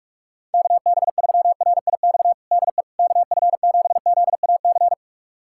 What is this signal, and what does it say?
Morse code, heard as a keyed tone, about 41 wpm, 700 Hz. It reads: KB3RIX DE KF7ZAC
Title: USA Callsign DE Callsign - 6 Characters - 45wpm